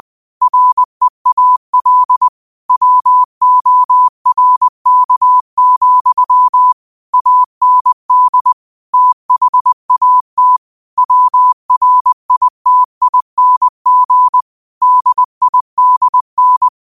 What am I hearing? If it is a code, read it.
Morse code, heard as a keyed tone, about 20 wpm, 1000 Hz. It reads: REAL WORK, AND THAT WRITING DIDN
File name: qotdfiles / 20wpm